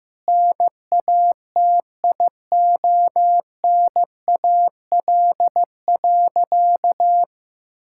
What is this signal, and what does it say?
Morse code, heard as a keyed tone, about 16 wpm, 700 Hz. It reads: NATIONAL.
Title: Sentences from Top 1000 Words - Review Part 5 - 15wpm